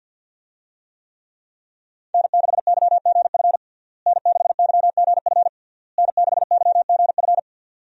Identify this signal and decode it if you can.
Morse code, heard as a keyed tone, about 53 wpm, 700 Hz. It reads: N6YGF N6YGF N6YGF
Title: USA Call Signs - 5 Characters - Repeated 3x - 50wpm